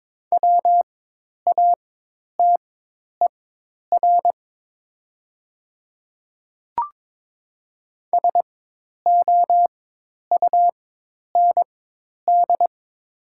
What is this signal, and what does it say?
Morse code, heard as a keyed tone, about 22 wpm, spaced as for 10 wpm, 700 Hz. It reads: WATER SOUND
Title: 4x Character Spacing - Top 200 Words - 22wpm